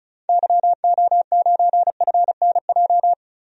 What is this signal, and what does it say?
Morse code, heard as a keyed tone, about 34 wpm, 700 Hz. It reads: YO9FNJ
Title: DXCC-100 Call Signs - 6 Characters - 35wpm